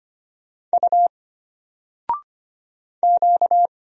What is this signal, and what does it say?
Morse code, heard as a keyed tone, about 25 wpm, 700 Hz. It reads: U Q
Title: Single Letter-Number Alt 25wpm